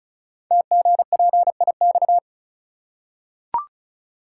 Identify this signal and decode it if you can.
Morse code, heard as a keyed tone, about 36 wpm, 690 Hz. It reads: TGPIX
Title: Code Groups - 5 Letters - 35wpm